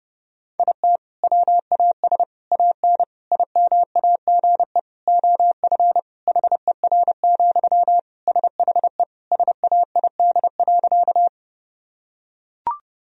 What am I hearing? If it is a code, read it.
Morse code, heard as a keyed tone, about 30 wpm, 700 Hz. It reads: IT WAS AN IMAGE OF HER, SHE SAID.
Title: Sentences from Top 1000 Words - Review Part 4 - 30wpm